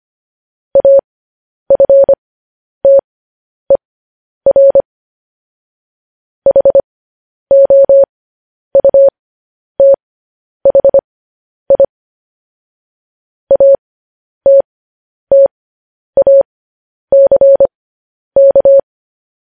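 Morse code, heard as a keyed tone, about 25 wpm, spaced as for 10 wpm, 550 Hz. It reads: AFTER HOUTHI ATTACK
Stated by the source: News headlines sent in Morse at 10 wpm.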